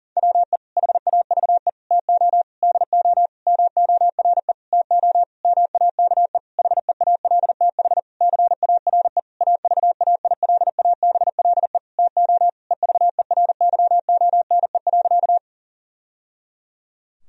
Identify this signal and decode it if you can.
Morse code, heard as a keyed tone, about 40 wpm, 700 Hz. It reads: WE HAVE TO DO MORE TO MAKE HEALTH CARE AVAILABLE TO EVERYONE.
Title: Sentences from Top 1000 Words - Review Part 3 - 40wpm